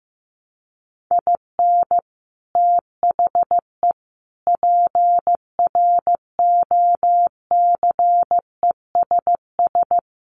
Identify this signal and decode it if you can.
Morse code, heard as a keyed tone, about 15 wpm, 700 Hz. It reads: IN THE PROCESS
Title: Sets of 3 Words 15wpm